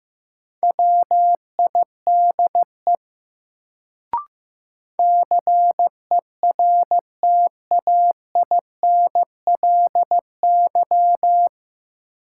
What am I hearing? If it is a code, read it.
Morse code, heard as a keyed tone, about 16 wpm, 700 Hz. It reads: WIDE CERTAINLY
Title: Sentences from Top 900 Words - New Word Drill - 15wpm